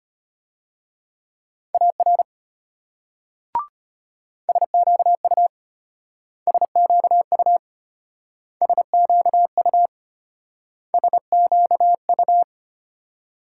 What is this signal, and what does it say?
Morse code, heard as a keyed tone, about 34 wpm, 700 Hz. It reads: AR SQU SQU SQU SQU